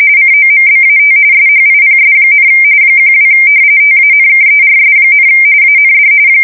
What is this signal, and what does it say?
RTTY, about 45.45 baud. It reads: COMBAT SHIPS, 12 TANKERS, AND
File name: fsk170 / battleshipnj